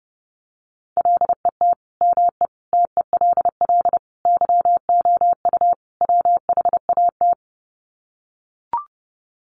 Morse code, heard as a keyed tone, about 28 wpm, 700 Hz. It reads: LET ME TELL YOU WHAT
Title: Sets of 5 Words Encore 30wpm